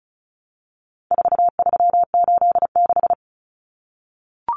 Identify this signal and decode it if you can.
Morse code, heard as a keyed tone, about 34 wpm, 700 Hz. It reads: 4386
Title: Common Cut Numbers - 4-Digit 35wpm